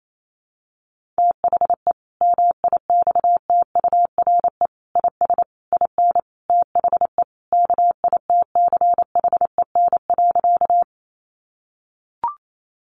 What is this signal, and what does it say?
Morse code, heard as a keyed tone, about 28 wpm, 700 Hz. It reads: THE MIXTURE IS IN THE KITCHEN.
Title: Sentences from Top 1500 Words 28wpm